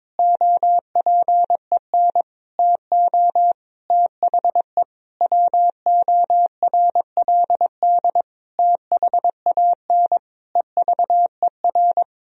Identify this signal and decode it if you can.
Morse code, heard as a keyed tone, about 23 wpm, 700 Hz. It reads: OPEN TO THE WORLD THAN EVER
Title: Sentences from Top 1000 Words - Review Part 3 - 22wpm